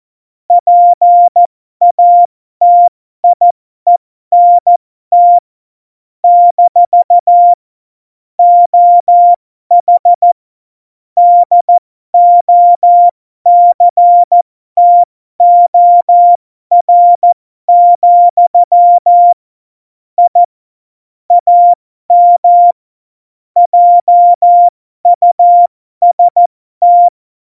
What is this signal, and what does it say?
Morse code, heard as a keyed tone, about 14 wpm, spaced as for 12 wpm, 700 Hz. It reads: PATIENT - OH DOCTOR, I AM JUST
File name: Morsekurs / Witze